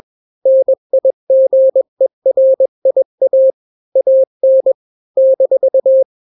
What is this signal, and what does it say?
Morse code, heard as a keyed tone, about 21 wpm, 530 Hz. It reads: NIGERIA AN -
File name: daily / 20wpm